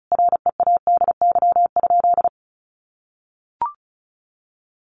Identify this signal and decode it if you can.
Morse code, heard as a keyed tone, about 34 wpm, 700 Hz. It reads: READY?
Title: Sentences from Top 1400 Words 35wpm